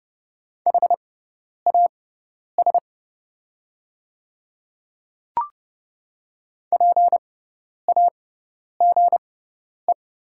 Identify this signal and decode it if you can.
Morse code, heard as a keyed tone, about 30 wpm, spaced as for 10 wpm, 700 Hz. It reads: HAS PAGE
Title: 6x Character Spacing - Top 300 Words - 30wpm